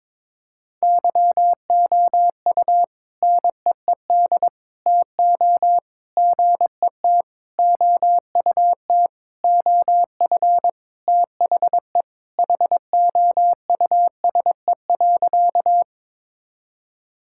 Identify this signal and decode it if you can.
Morse code, heard as a keyed tone, about 22 wpm, 700 Hz. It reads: YOU NEED TO GET OUT OF THE HOUSE.
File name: practice / 22wpm / Sentences with 4-Word Phrases - 22wpm